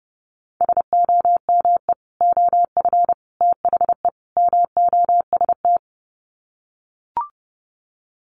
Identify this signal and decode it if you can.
Morse code, heard as a keyed tone, about 30 wpm, 700 Hz. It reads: SOME OF THE MOST
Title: Sets of 4 Words 30wpm